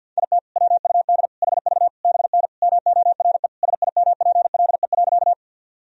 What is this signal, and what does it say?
Morse code, heard as a keyed tone, about 51 wpm, 700 Hz. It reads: IT WUD HV BN MORE SIMPLE.
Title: Sentences from Top 400 Words and Abbreviations 50wpm